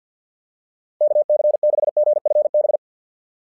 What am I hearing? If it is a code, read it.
Morse code, heard as a keyed tone, about 50 wpm, 660 Hz. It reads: KC6GFB
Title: SOTA QSO Practice 50wpm